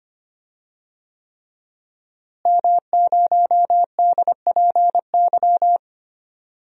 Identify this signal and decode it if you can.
Morse code, heard as a keyed tone, about 25 wpm, 700 Hz. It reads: M0DPY
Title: DXCC-100 Call Signs - 5 Characters - Repeated 3x - 25wpm